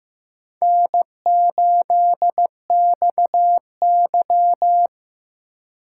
Morse code, heard as a keyed tone, about 15 wpm, 700 Hz. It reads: N8XY
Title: USA Call Signs - 4 Characters - 4x Character Spacing - 15wpm